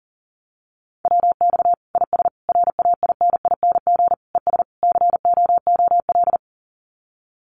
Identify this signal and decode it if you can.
Morse code, heard as a keyed tone, about 40 wpm, 700 Hz. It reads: WX IS RAINING ES COOL
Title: Common QSO Phrases 40wpm